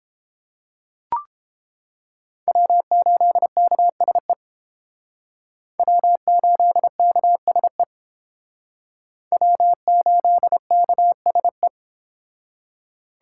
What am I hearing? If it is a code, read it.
Morse code, heard as a keyed tone, about 31 wpm, 700 Hz. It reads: W8KSE W8KSE W8KSE